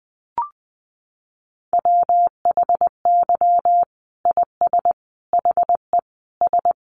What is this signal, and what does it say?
Morse code, heard as a keyed tone, about 20 wpm, 700 Hz. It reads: WHY IS HE S
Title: Sentences from Top 1200 Words 20wpm